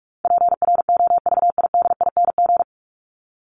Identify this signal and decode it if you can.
Morse code, heard as a keyed tone, about 44 wpm, 700 Hz. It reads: PROVIDING
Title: ING Suffix - Top 100 Words with Base Words 45wpm